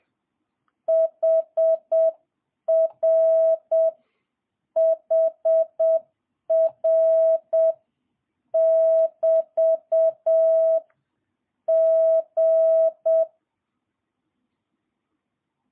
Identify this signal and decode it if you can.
Morse code, heard as a keyed tone, about 7 wpm, 650 Hz. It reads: HR HR <BT> G
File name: jh8cbh / 2022cw